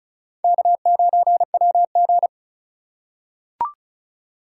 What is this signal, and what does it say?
Morse code, heard as a keyed tone, about 34 wpm, 700 Hz. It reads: K9WG
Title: USA Call Signs - 2x Character Spacing - 35wpm